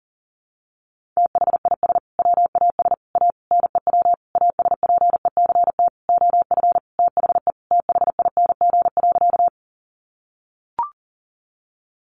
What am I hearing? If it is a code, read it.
Morse code, heard as a keyed tone, about 40 wpm, 700 Hz. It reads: THIS WAS A NEW ASPECT OF THE THING.
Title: Sentences from Top 1300 Words 40wpm